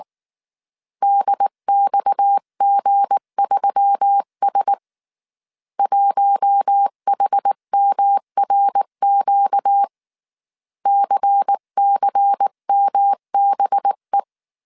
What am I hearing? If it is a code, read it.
Morse code, heard as a keyed tone, about 19 wpm, 800 Hz. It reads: DXG3S 1HMRQ CCMBE
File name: CW / DET / 1 9 1500C S C 18wpm 2539